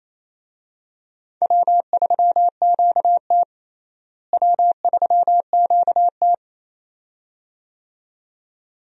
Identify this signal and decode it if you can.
Morse code, heard as a keyed tone, about 27 wpm, 700 Hz. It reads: W3QT W3QT
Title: USA Call Signs - 4 Characters - Repeated 2x - 28wpm